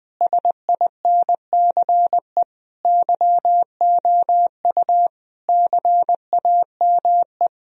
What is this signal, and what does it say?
Morse code, heard as a keyed tone, about 20 wpm, 700 Hz. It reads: SINCE YOU CAME
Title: Sentences from Top 1000 Words - Review Part 1 - 20wpm